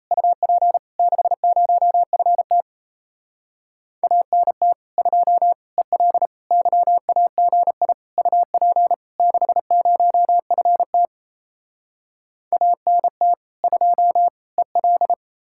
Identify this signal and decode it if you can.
Morse code, heard as a keyed tone, about 37 wpm, 700 Hz. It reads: UP 60FT ANT 2 EL YAGI UP 60FT ANT 2 EL